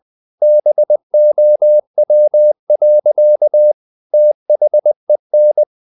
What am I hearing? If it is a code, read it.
Morse code, heard as a keyed tone, about 20 wpm, 600 Hz. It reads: BOW. THEN
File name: the old man and the sea part 2 20wpm